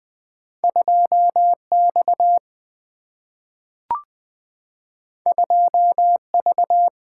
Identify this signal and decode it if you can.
Morse code, heard as a keyed tone, about 20 wpm, 700 Hz. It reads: 2 X 2 V
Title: Single-Letter Single-Number 20wpm